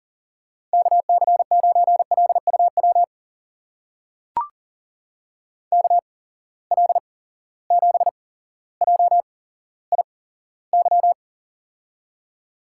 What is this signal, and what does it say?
Morse code, heard as a keyed tone, about 39 wpm, spaced as for 17 wpm, 700 Hz. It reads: KC9LUW KL7JIY
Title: USA Call Signs - 6 Characters - 8x Character Spacing - 40wpm